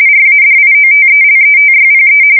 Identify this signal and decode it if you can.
RTTY, about 45.45 baud. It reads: RADIOTELETYPE
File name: fsk170 / teletype-RTTY